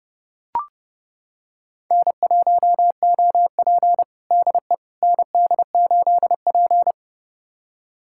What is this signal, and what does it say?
Morse code, heard as a keyed tone, about 30 wpm, 700 Hz. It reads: N1OP DE ND8P
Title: USA Callsign DE Callsign - 4 Characters - 30wpm